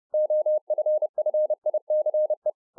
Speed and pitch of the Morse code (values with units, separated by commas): 30 wpm, 610 Hz